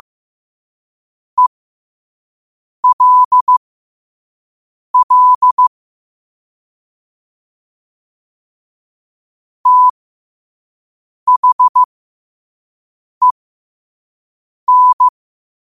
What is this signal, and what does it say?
Morse code, heard as a keyed tone, about 15 wpm, spaced as for 5 wpm, 1000 Hz. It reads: ELL THEN
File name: qotdfiles / 5wpm